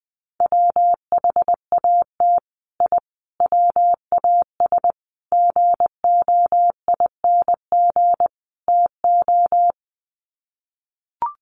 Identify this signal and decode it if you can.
Morse code, heard as a keyed tone, about 20 wpm, 700 Hz. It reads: WHAT I WAS GOING TO
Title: Sets of 5 Words Encore2 20wpm